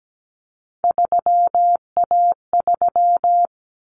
Morse code, heard as a keyed tone, about 17 wpm, 700 Hz. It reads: 3A3
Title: Common Cut Numbers - 3-Digit 17wpm